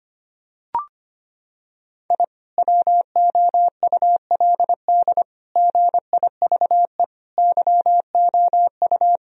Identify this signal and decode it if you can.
Morse code, heard as a keyed tone, about 25 wpm, 700 Hz. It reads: I WOULD GIVE YOU
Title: Sentences from Top 1000 Words - Review Part 1 - 25wpm